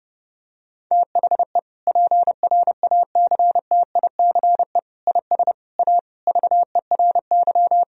Morse code, heard as a keyed tone, about 30 wpm, 700 Hz. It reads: THE PRACTICE IS A VERY
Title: Sentences from Top 1000 Words - Review Part 4 - 30wpm